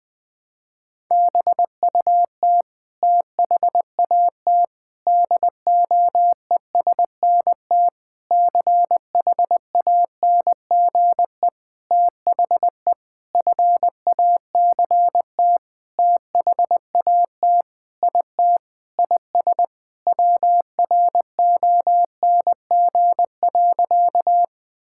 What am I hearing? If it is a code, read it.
Morse code, heard as a keyed tone, about 20 wpm, 700 Hz. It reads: BUT THAT DOESNT CHANGE THE FACT THAT IT IS WRONG.
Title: Sentences with 7-Word Phrases - 20wpm